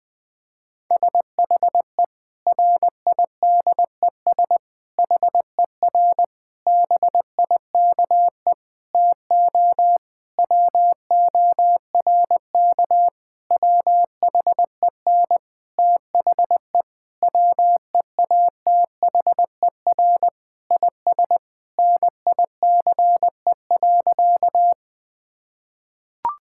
Morse code, heard as a keyed tone, about 20 wpm, 700 Hz. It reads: SHE RIDES HER BIKE TO WORK WHEN THE WEATHER IS NICE.
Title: Difficult Sentences with Hints - 20wpm